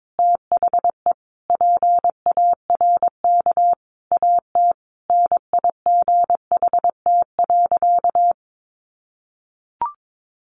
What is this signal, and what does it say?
Morse code, heard as a keyed tone, about 22 wpm, 700 Hz. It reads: THE PARK AT NIGHT.
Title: Sentences from Top 1300 Words 22wpm